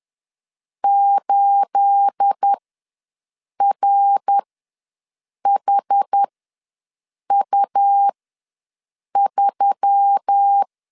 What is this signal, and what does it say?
Morse code, heard as a keyed tone, about 10 wpm, spaced as for 6 wpm, 800 Hz. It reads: 8RHU3
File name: LEZIONI / 10 SIGNS / 10A SLASH ws 10wpm 4121